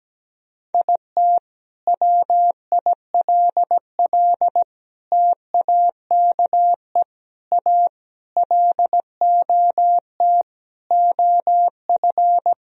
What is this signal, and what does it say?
Morse code, heard as a keyed tone, about 17 wpm, 700 Hz. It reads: IT WILL TAKE A LOT OF
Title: Sentences from Top 1000 Words - Review Part 4 - 17wpm